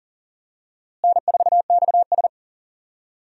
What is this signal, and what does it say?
Morse code, heard as a keyed tone, about 40 wpm, 700 Hz. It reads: N4XS
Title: USA Call Signs - 4 Characters - Repeated 2x - 40wpm